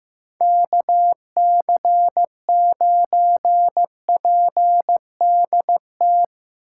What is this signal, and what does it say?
Morse code, heard as a keyed tone, about 15 wpm, 700 Hz. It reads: KC9PDT
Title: USA Call Signs - 6 Characters - 8x Character Spacing - 15wpm